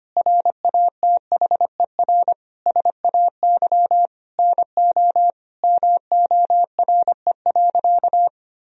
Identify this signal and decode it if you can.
Morse code, heard as a keyed tone, about 25 wpm, 700 Hz. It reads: RATHER SAY NO MORE.
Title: Sentences from Top 1000 Words - Review Part 4 - 25wpm